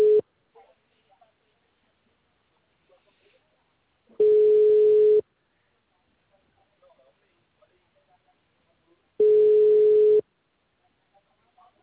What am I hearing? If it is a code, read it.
Ringback tone (Europe, 425 Hz).
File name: out- -2004